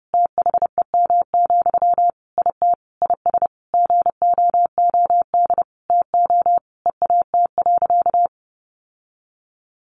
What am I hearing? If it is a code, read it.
Morse code, heard as a keyed tone, about 28 wpm, 700 Hz. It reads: THEM, IT IS GOOD TO EAT.
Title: Sentences from Top 800 Words 30wpm